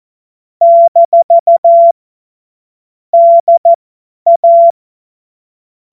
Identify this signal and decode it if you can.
Morse code, heard as a keyed tone, about 14 wpm, spaced as for 10 wpm, 680 Hz. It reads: - DA
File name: Morsekurs / Witze / Witz 29 April 2025 50 F70